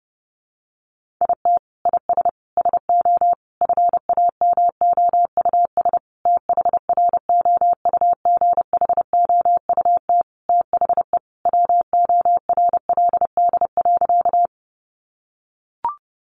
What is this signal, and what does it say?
Morse code, heard as a keyed tone, about 30 wpm, 700 Hz. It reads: IT IS SO FAMOUS THROUGHOUT THE WORLD.
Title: Sentences from Top 1500 Words 30wpm